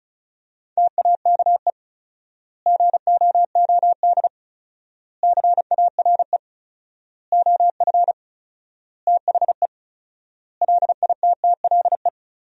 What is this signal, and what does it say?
Morse code, heard as a keyed tone, about 36 wpm, 700 Hz. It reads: TAKE GOOD CARE OF THE LITTLE
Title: 4x Word-Spacing - Sentences from Top 400 Words - 35wpm